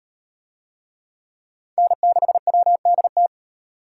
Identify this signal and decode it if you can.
Morse code, heard as a keyed tone, about 38 wpm, 700 Hz. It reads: N6WDT